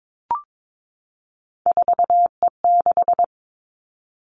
Morse code, heard as a keyed tone, about 22 wpm, 700 Hz. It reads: 4E6
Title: Three Challenging Letter-Number Combinations 22wpm